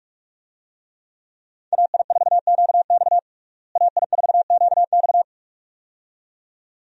Morse code, heard as a keyed tone, about 47 wpm, 700 Hz. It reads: AI4QX AI4QX
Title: USA Call Signs - 5 Characters - Repeated 2x - 45wpm